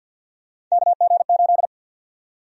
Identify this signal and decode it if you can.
Morse code, heard as a keyed tone, about 52 wpm, 700 Hz. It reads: KG8
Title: USA Call Sign Prefixes - 50wpm